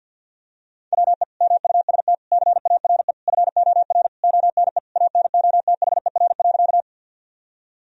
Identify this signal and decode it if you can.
Morse code, heard as a keyed tone, about 48 wpm, 700 Hz. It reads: WE MUST CARE FOR ONE ANOTHER.
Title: Sentences from Top 1000 Words - Review Part 1 - 50wpm